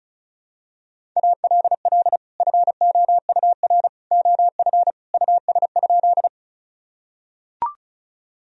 Morse code, heard as a keyed tone, about 34 wpm, 700 Hz. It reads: ALL FOUR OF US?
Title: Sentences from Top 1000 Words - Review Part 1 - 35wpm